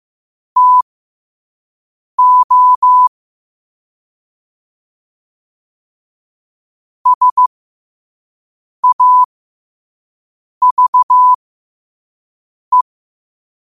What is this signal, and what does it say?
Morse code, heard as a keyed tone, about 15 wpm, spaced as for 5 wpm, 1000 Hz. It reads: TO SAVE
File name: qotdfiles / 5wpm